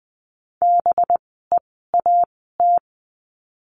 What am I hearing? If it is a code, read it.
Morse code, heard as a keyed tone, about 20 wpm, spaced as for 14 wpm, 700 Hz. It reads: BEAT
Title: 185 Lesson 30f - Letter b - Words 20wpm